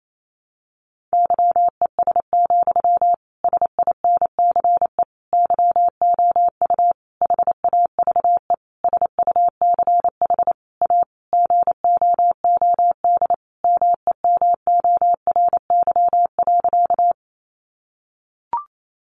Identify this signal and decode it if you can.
Morse code, heard as a keyed tone, about 28 wpm, 700 Hz. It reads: YES, SINCE YOU HAVE SUCH A GOOD MEMORY.
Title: Sentences from Top 1300 Words 28wpm